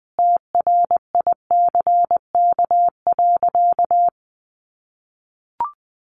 Morse code, heard as a keyed tone, about 20 wpm, 700 Hz. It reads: TRICK.
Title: Sentences from Top 1500 Words 20wpm